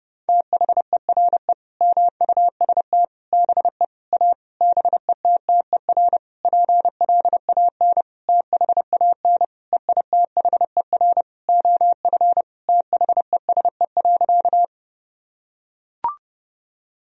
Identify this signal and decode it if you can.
Morse code, heard as a keyed tone, about 32 wpm, 700 Hz. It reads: THERE MUST BE A BETTER PLAN THAN EITHER OF THESE.
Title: Sentences from Top 1000 Words - Review Part 4 - 30wpm